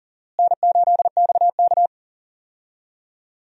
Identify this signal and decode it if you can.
Morse code, heard as a keyed tone, about 40 wpm, 700 Hz. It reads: N8XK
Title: USA Call Signs - 4 Characters - Repeated 2x - 40wpm